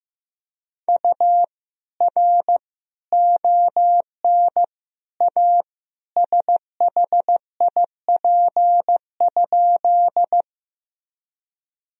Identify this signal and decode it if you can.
Morse code, heard as a keyed tone, about 15 wpm, 700 Hz. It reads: U R ON A SHIP?
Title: Sentences from Top 400 Words and Abbreviations 15wpm